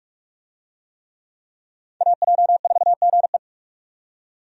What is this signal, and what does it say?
Morse code, heard as a keyed tone, about 46 wpm, 700 Hz. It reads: AJ4GE
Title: USA Call Signs - 5 Characters - 45wpm